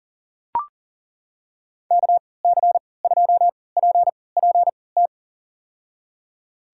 Morse code, heard as a keyed tone, about 40 wpm, spaced as for 23 wpm, 700 Hz. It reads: KC2PPT
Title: USA Call Signs - 6 Characters - 3x Character Spacing - 40wpm